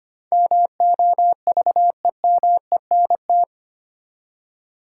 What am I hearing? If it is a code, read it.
Morse code, heard as a keyed tone, about 25 wpm, 700 Hz. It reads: MOVEMENT
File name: Eight Letter Words - 25wpm